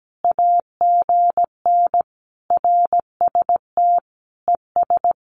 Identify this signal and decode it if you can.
Morse code, heard as a keyed tone, about 17 wpm, 700 Hz. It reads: AGN RST ES
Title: Common QSO Phrases 17wpm